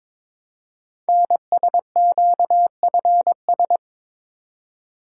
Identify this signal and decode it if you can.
Morse code, heard as a keyed tone, about 22 wpm, 700 Hz. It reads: NSQFS
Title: Code Groups - 5 Letters - 22wpm